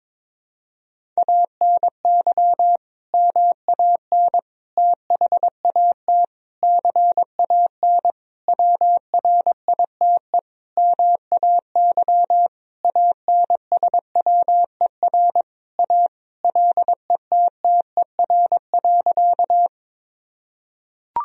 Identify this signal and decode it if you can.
Morse code, heard as a keyed tone, about 22 wpm, 700 Hz. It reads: ANY MAN THAT CAN WRITE MAY ANSWER A LETTER.
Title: Sentences from Top 1000 Words - Review Part 1 - 22wpm